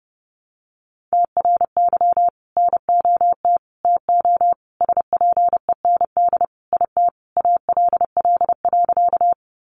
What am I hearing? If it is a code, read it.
Morse code, heard as a keyed tone, about 30 wpm, 700 Hz. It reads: TRY NOT TO SPEND IT ALL.
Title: Sentences from Top 800 Words 30wpm